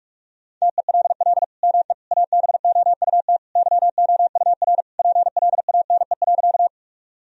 Morse code, heard as a keyed tone, about 41 wpm, 700 Hz. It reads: TELL ME ABOUT YOUR PLANE.
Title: Sentences from Top 1000 Words - Review Part 3 - 45wpm